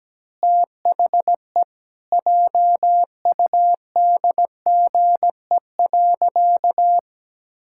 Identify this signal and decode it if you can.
Morse code, heard as a keyed tone, about 17 wpm, 700 Hz. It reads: THE JUDGE.
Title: Pangram Challenge v2 - 17wpm